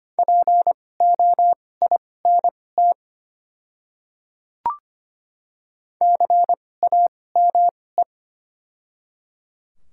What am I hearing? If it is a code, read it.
Morse code, heard as a keyed tone, about 25 wpm, spaced as for 18 wpm, 690 Hz. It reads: POINT CAME
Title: 2x Character Spacing - Top 200 Words - 25wpm